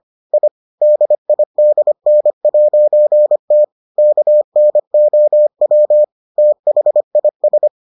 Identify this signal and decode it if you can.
Morse code, heard as a keyed tone, about 25 wpm, 600 Hz. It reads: I DIDN'T KNOW THIS